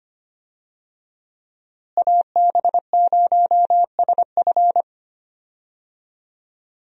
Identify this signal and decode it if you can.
Morse code, heard as a keyed tone, about 25 wpm, 700 Hz. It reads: AB0SF
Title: USA Call Signs - 5 Characters - Repeated 3x - 25wpm